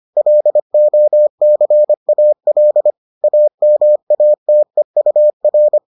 Morse code, heard as a keyed tone, about 25 wpm, 600 Hz. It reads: LOCAL AMATEUR